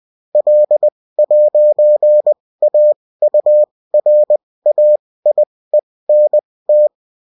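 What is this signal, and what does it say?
Morse code, heard as a keyed tone, about 20 wpm, spaced as for 16 wpm, 600 Hz. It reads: L'AURAIENT
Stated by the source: Morse code bulletin of a French amateur CW union, sent as text and in CW.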